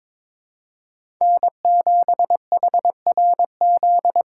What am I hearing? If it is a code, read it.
Morse code, heard as a keyed tone, about 22 wpm, 700 Hz. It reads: N7HRZ
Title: USA Call Signs - 5 Characters - 3x Character Spacing - 22wpm